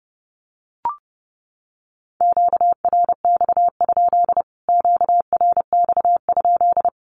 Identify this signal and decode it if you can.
Morse code, heard as a keyed tone, about 30 wpm, 700 Hz. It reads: QRX? QRX?
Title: Q-Codes Repeated 3x 30wpm